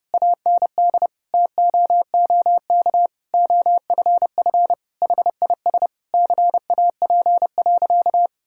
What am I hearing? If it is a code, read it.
Morse code, heard as a keyed tone, about 32 wpm, 700 Hz. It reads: AND TOOK OFF HIS CAP.
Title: Sentences from Top 1600 Words 30wpm